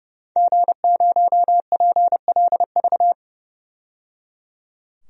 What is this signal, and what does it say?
Morse code, heard as a keyed tone, about 30 wpm, 700 Hz. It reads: G0PLV
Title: DXCC-100 Call Signs - 5 Characters - 30wpm